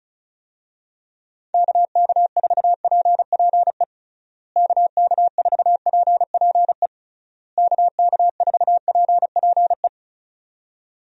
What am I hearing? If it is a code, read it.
Morse code, heard as a keyed tone, about 34 wpm, 700 Hz. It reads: KK4PPE KK4PPE KK4PPE
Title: USA Call Signs - 6 Characters - Repeated 3x - 35wpm